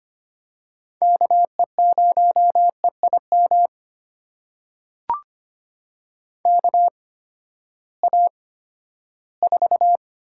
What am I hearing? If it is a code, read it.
Morse code, heard as a keyed tone, about 25 wpm, spaced as for 14 wpm, 700 Hz. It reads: KE0EIM KA4
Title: USA Call Signs - 6 Characters - 8x Character Spacing - 25wpm